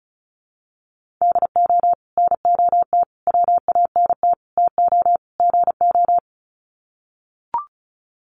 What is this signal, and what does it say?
Morse code, heard as a keyed tone, about 34 wpm, 700 Hz. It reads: DO NOT WANT TO GO
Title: Sets of 5 Words 35wpm